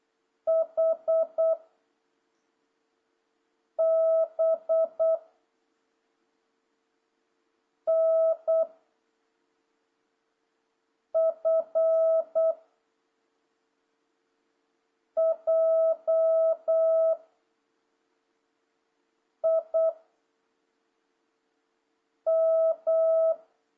Morse code, heard as a keyed tone, about 8 wpm, 650 Hz. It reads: H B N F J I M
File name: jh8cbh / 2026cw